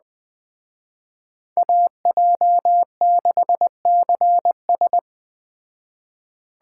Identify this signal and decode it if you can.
Morse code, heard as a keyed tone, about 20 wpm, 700 Hz. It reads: AJ6CS
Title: Speed Racing - USA 2x2 Call Signs 17wpm